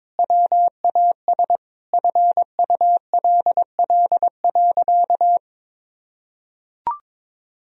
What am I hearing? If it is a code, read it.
Morse code, heard as a keyed tone, about 21 wpm, 700 Hz. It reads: WAS FULL.
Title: Sentences from Top 1000 Words - Review Part 3 - 22wpm